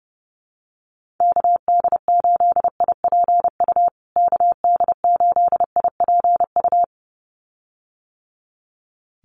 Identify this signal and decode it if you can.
Morse code, heard as a keyed tone, about 30 wpm, 700 Hz. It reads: KD8IPU KD8IPU
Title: USA Call Signs Repeated 2x 30wpm